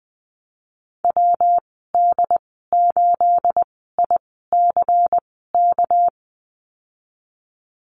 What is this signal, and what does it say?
Morse code, heard as a keyed tone, about 19 wpm, spaced as for 14 wpm, 700 Hz. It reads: WD8ICK
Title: 213 Lesson 34g - Number 8 - Call Signs 20wpm